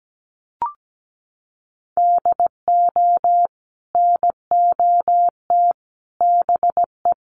Morse code, heard as a keyed tone, about 17 wpm, 700 Hz. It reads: DO NOT BE
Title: Sentences from Top 1500 Words 17wpm